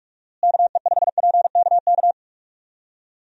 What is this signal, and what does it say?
Morse code, heard as a keyed tone, about 46 wpm, 700 Hz. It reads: KE5PKK
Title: USA Call Signs - 6 Characters - Repeated 2x - 45wpm